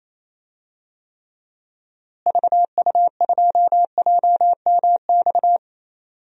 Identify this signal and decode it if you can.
Morse code, heard as a keyed tone, about 28 wpm, 700 Hz. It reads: VU2JMX
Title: DXCC-100 Call Signs - 6 Characters - Repeated 3x - 28wpm